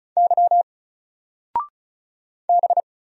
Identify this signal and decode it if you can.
Morse code, heard as a keyed tone, about 36 wpm, 700 Hz. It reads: Y B
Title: Single Letter-Number 35wpm